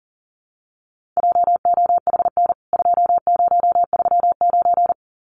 Morse code, heard as a keyed tone, about 40 wpm, 700 Hz. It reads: JOHN 2039
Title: CWT Exchange 40wpm